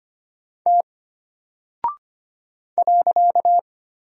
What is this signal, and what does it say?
Morse code, heard as a keyed tone, about 25 wpm, 700 Hz. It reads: T .
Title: Single Character (Letter:Number:Punctuation:Prosign) Alt 25wpm